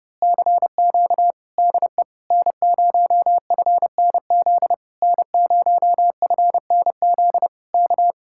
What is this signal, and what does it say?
Morse code, heard as a keyed tone, about 28 wpm, 700 Hz. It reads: CQ DE N0FNZ N0FNZ K
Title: Speed-Racing Common QSO Phrases - 30wpm